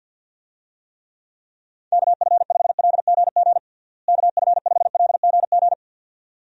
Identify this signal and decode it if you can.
Morse code, heard as a keyed tone, about 50 wpm, 700 Hz. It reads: KF5LGG KF5LGG
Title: USA Call Signs - 6 Characters - Repeated 2x - 50wpm